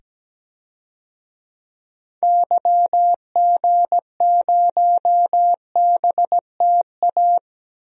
Morse code, heard as a keyed tone, about 17 wpm, 700 Hz. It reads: YG0BTA
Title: DXCC-100 Call Signs - 6 Characters - 17wpm